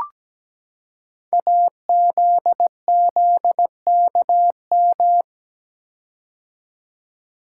Morse code, heard as a keyed tone, about 17 wpm, 700 Hz. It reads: AZZKM
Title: Code Groups - 5 Letters - 17wpm